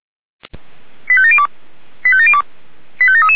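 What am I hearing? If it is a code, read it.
Five-tone selective call (ZVEI), heard as quick runs of five tones, tones of about 70 ms. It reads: 85702 85702 85733
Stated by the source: ZVEI five-tone call received by SDR, server of feuerwehr-orlamuende (fire brigade Orlamünde).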